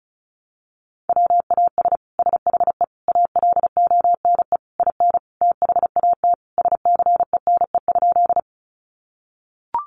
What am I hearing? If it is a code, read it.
Morse code, heard as a keyed tone, about 34 wpm, 700 Hz. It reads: WAS SHE ALONE IN THAT SCENE?
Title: Sentences from Top 1100 Words 35wpm